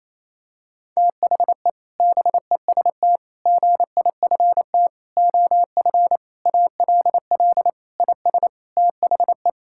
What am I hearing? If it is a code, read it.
Morse code, heard as a keyed tone, about 28 wpm, 700 Hz. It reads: THE BEST GIFT OF ALL IS THE
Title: Sentences from Top 1000 Words - Review Part 6 - 28wpm